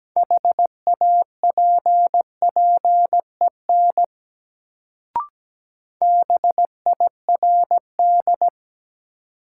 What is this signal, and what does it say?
Morse code, heard as a keyed tone, about 17 wpm, 700 Hz. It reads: HAPPEN BIRD
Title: Sentences from Top 400 Words - New Word Drill - 17wpm